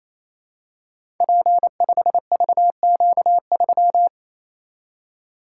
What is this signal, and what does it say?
Morse code, heard as a keyed tone, about 27 wpm, 700 Hz. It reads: P5VQ3
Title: Code Groups - 5 Letter:Numbers - 28wpm